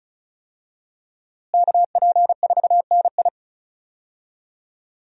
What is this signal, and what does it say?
Morse code, heard as a keyed tone, about 36 wpm, 700 Hz. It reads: KP4NI
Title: USA Call Signs - 5 Characters - Repeated 3x - 35wpm